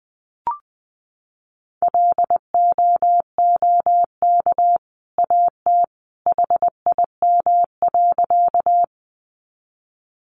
Morse code, heard as a keyed tone, about 19 wpm, 700 Hz. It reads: LOOK AT HIM.
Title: Sentences from Top 100 Words 20wpm